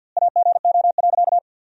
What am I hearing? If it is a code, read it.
Morse code, heard as a keyed tone, about 50 wpm, 700 Hz. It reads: AGO.
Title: 2x Word-Spacing - Sentences from Top 500 Words - 50wpm